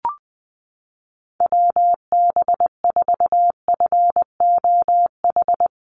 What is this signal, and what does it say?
Morse code, heard as a keyed tone, about 20 wpm, 700 Hz. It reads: WB4FOH
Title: USA Call Signs Repeated 3x 20wpm